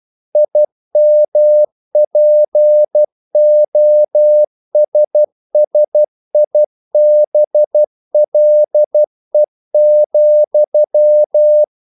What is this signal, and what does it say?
Morse code, heard as a keyed tone, about 12 wpm, 600 Hz. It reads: IMPOSSIBLE,
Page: English quotations in CW (telegraphy - morse)